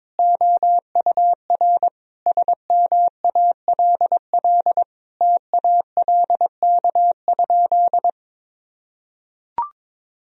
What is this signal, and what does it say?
Morse code, heard as a keyed tone, about 22 wpm, 700 Hz. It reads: OUR SMALL TALK?
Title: Sentences from Top 1000 Words - Review Part 1 - 22wpm